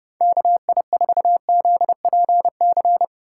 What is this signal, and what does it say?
Morse code, heard as a keyed tone, about 30 wpm, 700 Hz. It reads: KI4ZPC
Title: USA Call Signs - 6 Characters - 4x Character Spacing - 30wpm